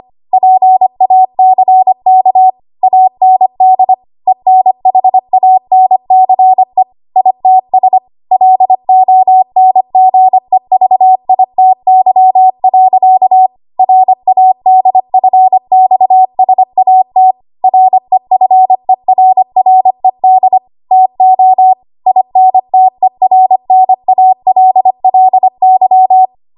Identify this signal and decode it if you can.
Morse code, heard as a keyed tone, about 25 wpm, 750 Hz. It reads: PACK AND ENHANCE ITS LONGEVITY. RADFXSAT REFERRED TO INTERNALLY
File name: Archive / 25 WPM / 25WPM